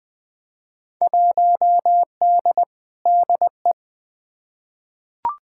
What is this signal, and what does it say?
Morse code, heard as a keyed tone, about 20 wpm, 700 Hz. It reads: 1D DE
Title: Field Day Exchanges - 20wpm